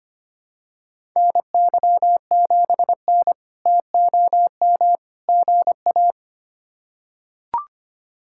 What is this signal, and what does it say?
Morse code, heard as a keyed tone, about 25 wpm, 700 Hz. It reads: NY7N TOM GA
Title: K1USN SST QSO Practice - 25wpm